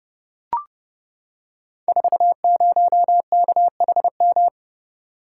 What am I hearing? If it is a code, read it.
Morse code, heard as a keyed tone, about 30 wpm, 700 Hz. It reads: 40KHM
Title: Code Groups - 5 Letter:Numbers - 30wpm